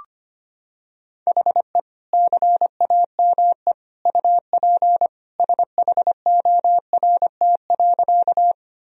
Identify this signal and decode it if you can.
Morse code, heard as a keyed tone, about 25 wpm, 700 Hz. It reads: HE CAME UP SHORT.
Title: Sentences from Top 1000 Words - Review Part 2 - 25wpm